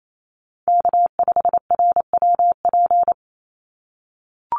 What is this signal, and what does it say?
Morse code, heard as a keyed tone, about 28 wpm, 700 Hz. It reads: K5RWP
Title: NA SOTA Chaser Callsigns 28wpm